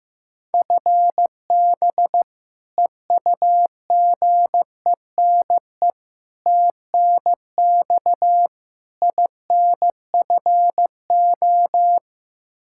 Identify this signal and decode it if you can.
Morse code, heard as a keyed tone, about 16 wpm, 700 Hz. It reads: FB EUGENE TNX INFO
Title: Practice QSO - Exchange 3 - More Info - 15wpm